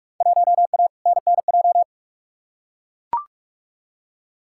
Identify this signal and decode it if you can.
Morse code, heard as a keyed tone, about 43 wpm, 700 Hz. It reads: 1A NNJ
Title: Field Day Exchanges Encore - 45wpm